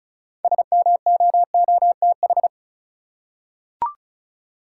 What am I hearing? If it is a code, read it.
Morse code, heard as a keyed tone, about 34 wpm, 700 Hz. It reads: SMOOTH
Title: Six Letter Words - 35wpm